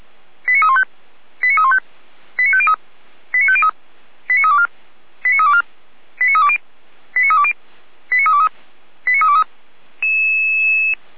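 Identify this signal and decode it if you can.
Five-tone selective call (ZVEI), heard as quick runs of five tones, tones of about 70 ms. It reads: 89315 89315 89503 89503 89324 89324 89320 89320 89323 89323
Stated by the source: ZVEI five-tone call received by SDR.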